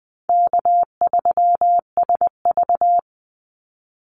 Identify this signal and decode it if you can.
Morse code, heard as a keyed tone, about 19 wpm, 700 Hz. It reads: K3SV
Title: Easy Call Signs Repeated 2x - 20wpm